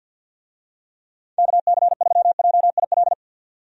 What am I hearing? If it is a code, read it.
Morse code, heard as a keyed tone, about 52 wpm, 700 Hz. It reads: KC3JIL
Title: USA Call Signs - 6 Characters - 50wpm